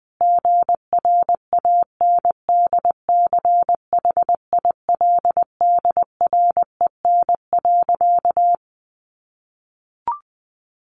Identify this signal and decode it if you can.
Morse code, heard as a keyed tone, about 20 wpm, 700 Hz. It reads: GRANDCHILDREN.
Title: A Sound Of Thunder 20wpm